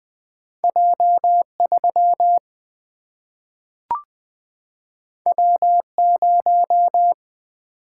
Morse code, heard as a keyed tone, about 20 wpm, 700 Hz. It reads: J 3 W 0
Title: Single-Letter Single-Number 20wpm